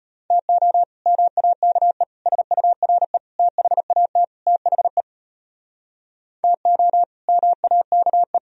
Morse code, heard as a keyed tone, about 32 wpm, 700 Hz. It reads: TO MAKE SURE THAT THE TO MAKE